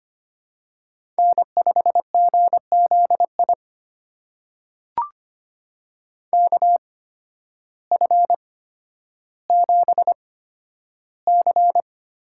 Morse code, heard as a keyed tone, about 25 wpm, spaced as for 12 wpm, 700 Hz. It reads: N5GZI KF7C
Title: USA Call Signs - 5 Characters - 8x Character Spacing - 25wpm